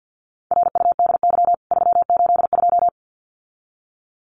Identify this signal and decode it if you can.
Morse code, heard as a keyed tone, about 50 wpm, 700 Hz. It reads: RUDY 382
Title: CWT Exchange 50wpm